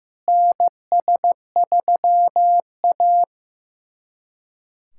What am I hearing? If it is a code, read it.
Morse code, heard as a keyed tone, about 15 wpm, 700 Hz. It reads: NS3A
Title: USA Callsign DE Callsign - 4 Characters - 15wpm